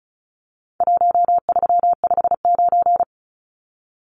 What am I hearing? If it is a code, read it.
Morse code, heard as a keyed tone, about 34 wpm, 700 Hz. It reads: 1359
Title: Numbers - Contest Style - 4-Digit 35wpm